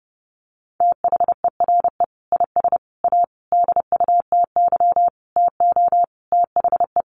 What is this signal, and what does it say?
Morse code, heard as a keyed tone, about 30 wpm, 700 Hz. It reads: THERE IS A DUTY TO THE
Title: Sentences from Top 1500 Words 30wpm